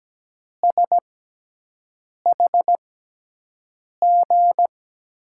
Morse code, heard as a keyed tone, about 17 wpm, spaced as for 6 wpm, 700 Hz. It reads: SHG
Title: USA Call Signs - 6x Character Spacing - 17wpm